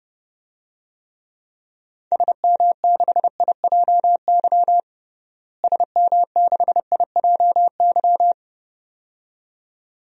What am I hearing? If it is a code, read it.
Morse code, heard as a keyed tone, about 30 wpm, 700 Hz. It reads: SM6IJY SM6IJY
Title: DXCC-100 Call Signs - 6 Characters - Repeated 2x - 30wpm